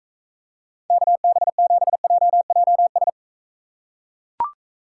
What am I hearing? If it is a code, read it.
Morse code, heard as a keyed tone, about 38 wpm, 700 Hz. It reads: KB7JJS